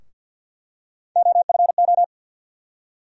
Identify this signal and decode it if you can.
Morse code, heard as a keyed tone, about 53 wpm, 700 Hz. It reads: OFO
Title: USA Call Sign Postfixes - 3 Letters 50wpm